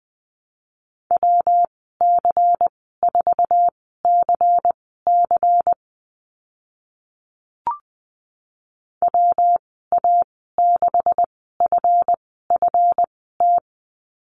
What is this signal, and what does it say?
Morse code, heard as a keyed tone, about 20 wpm, spaced as for 14 wpm, 700 Hz. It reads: WC4CC WA6FFT
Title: 154 Lesson 25g - Letter p - Call Signs 20wpm